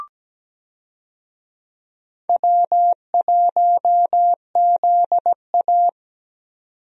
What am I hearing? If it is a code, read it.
Morse code, heard as a keyed tone, about 17 wpm, 700 Hz. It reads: W1ZA
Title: USA Call Signs - 4 Characters - 17wpm